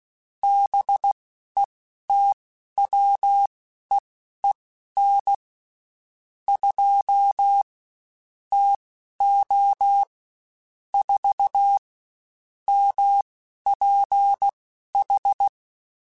Morse code, heard as a keyed tone, about 16 wpm, spaced as for 11 wpm, 800 Hz. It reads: BETWEEN 2 TO 4 MPH
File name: Spaced Characters / Spaced Characters-mp3s / Weather 4 14 sp @15